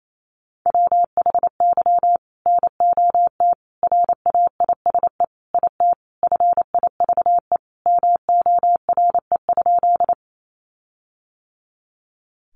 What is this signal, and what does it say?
Morse code, heard as a keyed tone, about 28 wpm, 700 Hz. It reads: WHY NOT RAISE IT FIVE MORE?
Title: Sentences from Top 1400 Words 28wpm